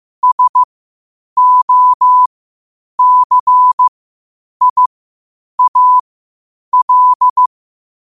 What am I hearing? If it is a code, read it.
Morse code, heard as a keyed tone, about 15 wpm, spaced as for 8 wpm, 1000 Hz. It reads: SOCIAL
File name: qotdfiles / 8wpm